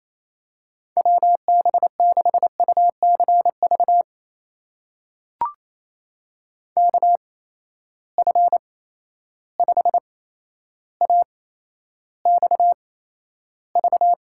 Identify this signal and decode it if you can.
Morse code, heard as a keyed tone, about 28 wpm, spaced as for 12 wpm, 700 Hz. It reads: WB6UCV KF5AXV
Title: USA Call Signs - 6 Characters - 8x Character Spacing - 28wpm